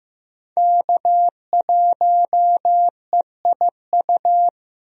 Morse code, heard as a keyed tone, about 15 wpm, 700 Hz. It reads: K1EIU
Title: USA Call Signs - 5 Characters - 15wpm